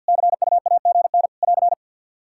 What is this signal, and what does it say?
Morse code, heard as a keyed tone, about 50 wpm, 700 Hz. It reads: CUAGN <AR>
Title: Practice QSO - Exchange 4 - Ending - 50wpm